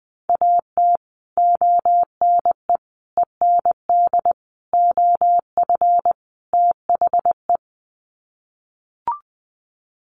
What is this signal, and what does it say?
Morse code, heard as a keyed tone, about 20 wpm, 700 Hz. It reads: AT ONE END OF THE
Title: Sets of 5 Words Encore 20wpm